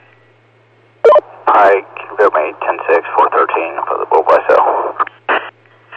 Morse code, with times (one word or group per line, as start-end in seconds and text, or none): none